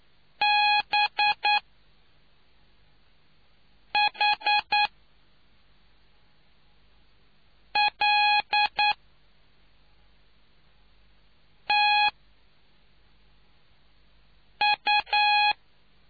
Morse code, heard as a keyed tone, about 9 wpm, spaced as for 3 wpm, 830 Hz. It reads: BHLTU